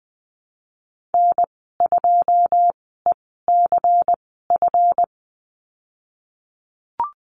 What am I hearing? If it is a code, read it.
Morse code, heard as a keyed tone, about 20 wpm, spaced as for 15 wpm, 700 Hz. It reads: N2ECF
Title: 140 Lesson 23g - Letter f - Call Signs 20wpm